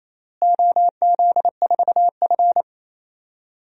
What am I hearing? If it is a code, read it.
Morse code, heard as a keyed tone, about 28 wpm, 700 Hz. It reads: OZ4F
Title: DXCC-100 Call Signs - 4 Characters - Repeated 3x - 28wpm